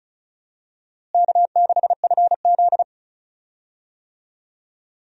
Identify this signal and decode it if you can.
Morse code, heard as a keyed tone, about 36 wpm, 700 Hz. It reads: K6FZ
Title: USA Call Signs - 4 Characters - Repeated 2x - 35wpm